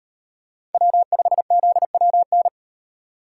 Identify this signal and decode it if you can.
Morse code, heard as a keyed tone, about 37 wpm, 700 Hz. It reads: W5ZWN